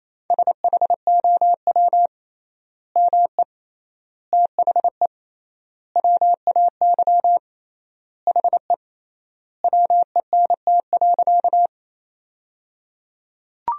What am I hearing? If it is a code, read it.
Morse code, heard as a keyed tone, about 27 wpm, 700 Hz. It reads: SHOW ME THE WAY HE WENT.
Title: 3x Word-Spacing - Sentences from Top 200 Words - 28wpm